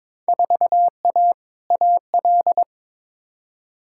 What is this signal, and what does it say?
Morse code, heard as a keyed tone, about 23 wpm, 700 Hz. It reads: 4A AL
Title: Field Day Exchanges - 22wpm